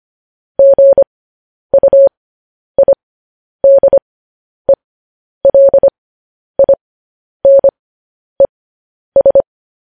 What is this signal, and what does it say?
Morse code, heard as a keyed tone, about 25 wpm, spaced as for 10 wpm, 550 Hz. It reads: GUIDELINES
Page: news headlines in morse code at 10 wpm